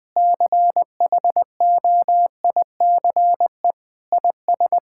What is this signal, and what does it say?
Morse code, heard as a keyed tone, about 19 wpm, 700 Hz. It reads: CHOICE IS
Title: Sentences from Top 1000 Words - Review Part 6 - 20wpm